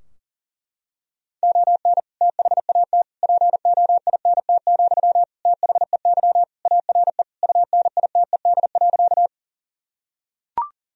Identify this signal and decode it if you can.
Morse code, heard as a keyed tone, about 40 wpm, 700 Hz. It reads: ON THAT POINT, THEY ARE UNITED.
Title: Sentences from Top 1000 Words - Review Part 4 - 40wpm